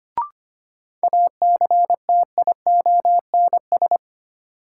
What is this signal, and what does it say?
Morse code, heard as a keyed tone, about 25 wpm, 700 Hz. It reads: ACTIONS
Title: Sentences from Top 1000 Words - Review Part 6 - New Word Drill - 25wpm